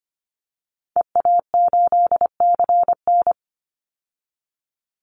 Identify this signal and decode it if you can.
Morse code, heard as a keyed tone, about 25 wpm, 700 Hz. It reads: EA8CN
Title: Call Signs Repeated 3x - 25wpm